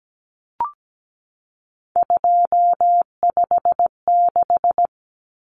Morse code, heard as a keyed tone, about 17 wpm, 700 Hz. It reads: 256
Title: Common Cut Numbers - 3-Digit 17wpm